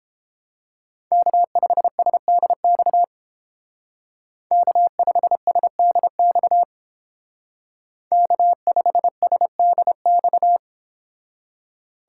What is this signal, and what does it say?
Morse code, heard as a keyed tone, about 31 wpm, 700 Hz. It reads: K5SDX K5SDX K5SDX